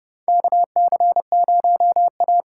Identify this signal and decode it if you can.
Morse code, heard as a keyed tone, about 30 wpm, 700 Hz. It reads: KC0A
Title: USA Call Signs - 6 Characters - 6x Character Spacing - 30wpm